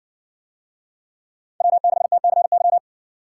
Morse code, heard as a keyed tone, about 60 wpm, 700 Hz. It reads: W6TCY